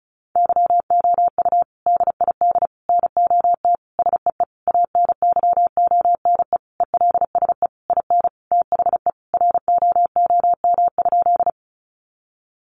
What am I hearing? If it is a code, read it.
Morse code, heard as a keyed tone, about 34 wpm, 700 Hz. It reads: YOU DID NOT SEE ANYONE ELSE IN THE ROOM?
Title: Sentences from Top 900 Words 35wpm